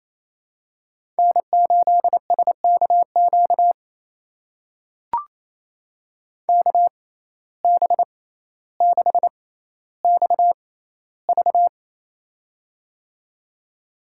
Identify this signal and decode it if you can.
Morse code, heard as a keyed tone, about 28 wpm, spaced as for 14 wpm, 700 Hz. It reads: N8SKQ KB6XV
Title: USA Call Signs - 5 Characters - 6x Character Spacing - 28wpm